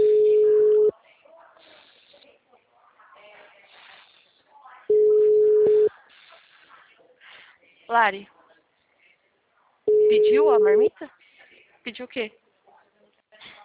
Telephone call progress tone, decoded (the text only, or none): ringback tone